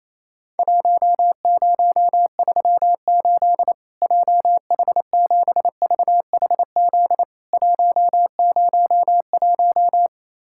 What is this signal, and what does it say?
Morse code, heard as a keyed tone, about 28 wpm, 700 Hz. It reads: 1038 JH7VHZ 101
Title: Worked All Europe DX Contest - QTC Exchange - 28wpm